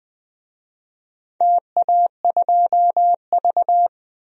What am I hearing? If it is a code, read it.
Morse code, heard as a keyed tone, about 20 wpm, 700 Hz. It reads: TA2V
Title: DXCC-100 Call Signs - 4 Characters - 20wpm